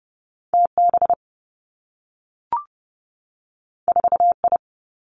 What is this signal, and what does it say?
Morse code, heard as a keyed tone, about 30 wpm, 700 Hz. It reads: T B 4 I
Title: Two Challenging Letter-Number Combinations 30wpm